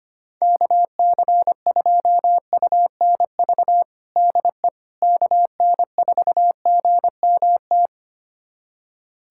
Morse code, heard as a keyed tone, about 25 wpm, 700 Hz. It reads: KC2UNV DE KN4GMT
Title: USA Callsign DE Callsign - 6 Characters - 25wpm